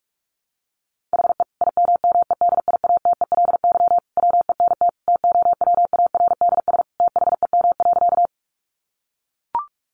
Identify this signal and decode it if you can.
Morse code, heard as a keyed tone, about 44 wpm, 700 Hz. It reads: HE IMMEDIATELY WENT TOWARDS THEM.
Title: Sentences from Top 1100 Words 45wpm